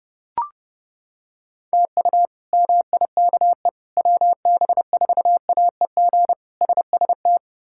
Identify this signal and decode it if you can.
Morse code, heard as a keyed tone, about 32 wpm, 700 Hz. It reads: TU MIKE WB4AEG SST
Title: K1USN SST QSO Practice - 30wpm